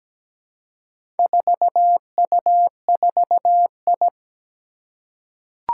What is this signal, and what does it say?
Morse code, heard as a keyed tone, about 17 wpm, 700 Hz. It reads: 4U4I
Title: Four Challenging Letter-Number Combinations 17wpm